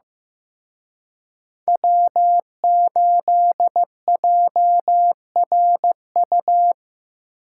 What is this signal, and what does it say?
Morse code, heard as a keyed tone, about 18 wpm, 700 Hz. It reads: W8JRU
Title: Speed Racing - USA 1x3 Call Signs 15wpm